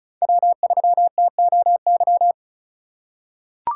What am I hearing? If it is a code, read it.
Morse code, heard as a keyed tone, about 36 wpm, 700 Hz. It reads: W3TOY
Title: USA Call Signs - 2x Character Spacing - 35wpm